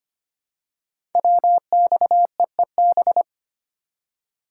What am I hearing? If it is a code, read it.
Morse code, heard as a keyed tone, about 25 wpm, 690 Hz. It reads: WXEEB
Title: Code Groups - 5 Letters - 25wpm